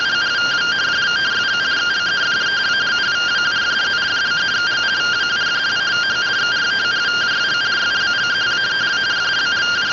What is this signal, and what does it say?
Two-tone teleprinter signal, about 170 Hz shift.